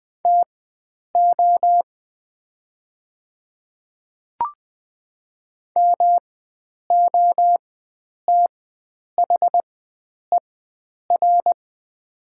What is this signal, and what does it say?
Morse code, heard as a keyed tone, about 20 wpm, spaced as for 9 wpm, 700 Hz. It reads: TO MOTHER
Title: 4x Character Spacing - Top 300 Words - 20wpm